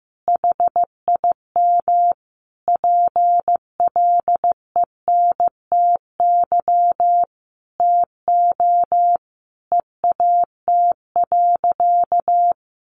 Morse code, heard as a keyed tone, about 15 wpm, 700 Hz. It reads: HIM PLENTY TO EAT.
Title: Sentences from Top 1300 Words 15wpm